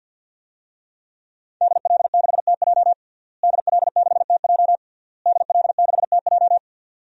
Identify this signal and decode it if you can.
Morse code, heard as a keyed tone, about 52 wpm, 700 Hz. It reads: DL6TJ DL6TJ DL6TJ
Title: DXCC-100 Call Signs - 5 Characters - Repeated 3x - 50wpm